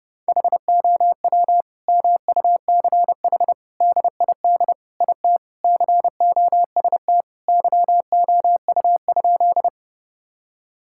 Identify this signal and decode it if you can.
Morse code, heard as a keyed tone, about 30 wpm, 700 Hz. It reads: HOW MUCH DID IT COST YOU?
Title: Sentences from Top 1000 Words - Review Part 5 - 30wpm